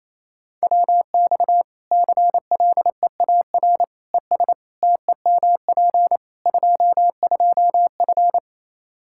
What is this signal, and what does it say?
Morse code, heard as a keyed tone, about 28 wpm, 700 Hz. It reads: WX CLEAR ES TEMP 22F
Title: Practice QSO - Exchange 2 - Info - 28wpm